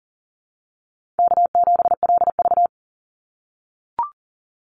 Morse code, heard as a keyed tone, about 40 wpm, 700 Hz. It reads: K7LV
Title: Call Signs - 40wpm